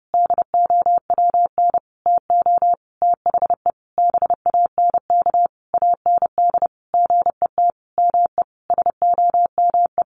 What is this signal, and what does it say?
Morse code, heard as a keyed tone, about 30 wpm, 700 Hz. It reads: DOWN TO THE BANK AND GET ME SOME
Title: Sentences from Top 1500 Words 30wpm